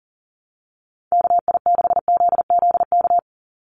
Morse code, heard as a keyed tone, about 40 wpm, 700 Hz. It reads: KI6ZZK
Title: USA Call Signs 40wpm